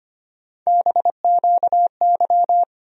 Morse code, heard as a keyed tone, about 25 wpm, 700 Hz. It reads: BQY
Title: USA Call Sign Postfixes - 3 Letters 25wpm